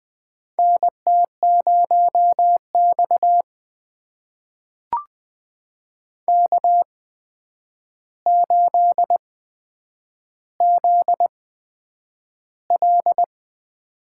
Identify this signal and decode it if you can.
Morse code, heard as a keyed tone, about 20 wpm, spaced as for 9 wpm, 700 Hz. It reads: NT0X K8ZL
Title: USA Call Signs - 4 Characters - 8x Character Spacing - 20wpm